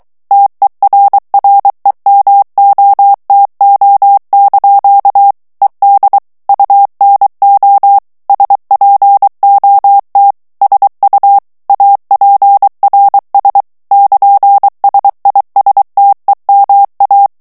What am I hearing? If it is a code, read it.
Morse code, heard as a keyed tone, about 23 wpm, 800 Hz. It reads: TERREMOTO) ED UNO SPOT SU APRS <KN>SISTEMA
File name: morse mp3 / ITALIEN texte clair 1 vitesse 22 mots minute